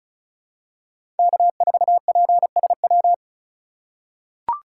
Morse code, heard as a keyed tone, about 34 wpm, 700 Hz. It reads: K4PSW
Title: USA Call Signs - 2x Character Spacing - 35wpm